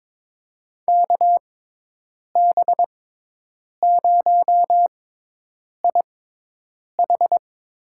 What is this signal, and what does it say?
Morse code, heard as a keyed tone, about 22 wpm, spaced as for 8 wpm, 700 Hz. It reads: KB0IH
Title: USA Call Signs - 6 Characters - 6x Character Spacing - 22wpm